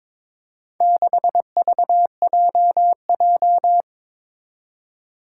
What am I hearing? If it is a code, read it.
Morse code, heard as a keyed tone, about 21 wpm, 700 Hz. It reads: 6VJJ
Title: Four Challenging Letter-Number Combinations 22wpm